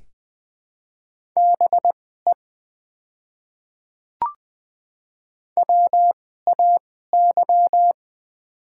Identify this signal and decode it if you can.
Morse code, heard as a keyed tone, about 20 wpm, spaced as for 14 wpm, 690 Hz. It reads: BE WAY
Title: 2x Character Spacing - Top 100 Words - 20wpm